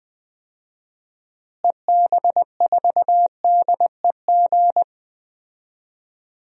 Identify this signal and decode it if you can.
Morse code, heard as a keyed tone, about 20 wpm, 700 Hz. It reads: EB4DEG
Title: DXCC-100 Call Signs - 6 Characters - Repeated 2x - 20wpm